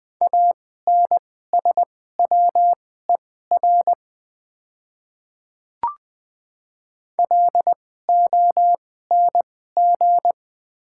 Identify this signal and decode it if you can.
Morse code, heard as a keyed tone, about 20 wpm, spaced as for 14 wpm, 690 Hz. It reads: ANSWER LONG
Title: 2x Character Spacing - Top 300 Words - 20wpm